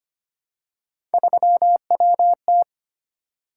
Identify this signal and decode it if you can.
Morse code, heard as a keyed tone, about 25 wpm, 700 Hz. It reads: 3WT
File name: Three Letter-Number Combinations - 25wpm